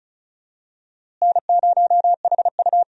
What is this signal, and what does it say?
Morse code, heard as a keyed tone, about 34 wpm, 700 Hz. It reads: N0HU
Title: USA Call Signs - 4 Characters - Repeated 2x - 35wpm